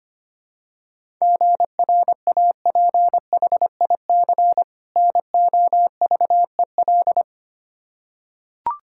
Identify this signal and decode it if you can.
Morse code, heard as a keyed tone, about 25 wpm, 700 Hz. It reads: GRAPHIC NOVEL
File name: practice / 25wpm / Difficult Words with Hints - 25wpm